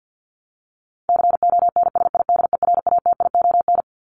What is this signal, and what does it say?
Morse code, heard as a keyed tone, about 50 wpm, 700 Hz. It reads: CONSIDERATION
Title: Top 1251-1500 Words 50wpm